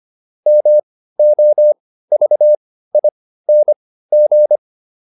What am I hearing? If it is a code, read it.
Morse code, heard as a keyed tone, about 25 wpm, spaced as for 15 wpm, 600 Hz. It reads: MOVING
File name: SS215 15